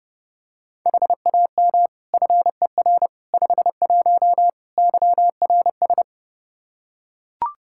Morse code, heard as a keyed tone, about 32 wpm, 700 Hz. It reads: HAM FER 51 YRS
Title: Practice QSO - Exchange 3 - More Info - 30wpm